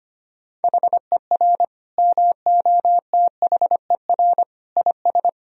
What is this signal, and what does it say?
Morse code, heard as a keyed tone, about 25 wpm, 700 Hz. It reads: HER MOTHER IS
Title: Sentences from Top 1000 Words - Review Part 3 - 25wpm